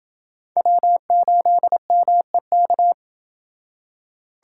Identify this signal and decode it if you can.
Morse code, heard as a keyed tone, about 29 wpm, 700 Hz. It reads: W8MEK